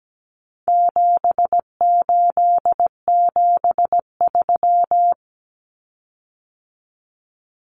Morse code, heard as a keyed tone, about 17 wpm, 700 Hz. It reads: 7873
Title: Common Cut Numbers - 4-Digit 17wpm